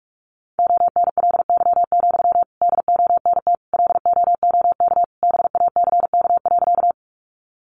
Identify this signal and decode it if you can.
Morse code, heard as a keyed tone, about 44 wpm, 700 Hz. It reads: ONLY, DONT LOOK BACK.
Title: Sentences from Top 300 Words 45wpm